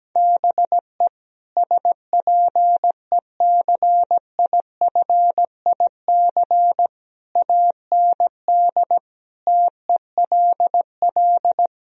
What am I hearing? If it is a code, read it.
Morse code, heard as a keyed tone, about 17 wpm, 700 Hz. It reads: BE SPECIFIC AND TELL
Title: Sentences from Top 1000 Words - Review Part 3 - 17wpm